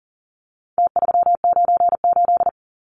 Morse code, heard as a keyed tone, about 40 wpm, 700 Hz. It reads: T398
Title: Numbers - Contest Style - 4-Digit 40wpm